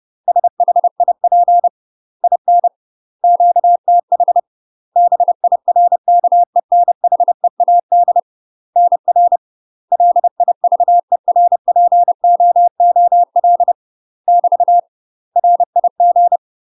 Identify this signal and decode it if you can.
Morse code, heard as a keyed tone, about 30 wpm, 700 Hz. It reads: SHIP IN QTH BIRKENHEAD NR LIVERPOOL <BT> RIG
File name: UK MASTER QSOs TRAINING / 30WPM / UK MASTER QSOs 0021